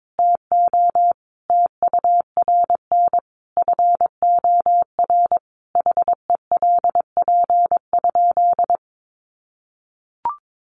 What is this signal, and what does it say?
Morse code, heard as a keyed tone, about 22 wpm, 700 Hz. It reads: TO TURN FOR HELP?
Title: Sentences from Top 200 Words 22wpm